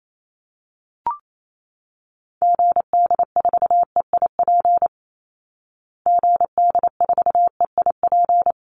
Morse code, heard as a keyed tone, about 28 wpm, 690 Hz. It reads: GD4EIP GD4EIP
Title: Call Signs Repeated 3x - 28wpm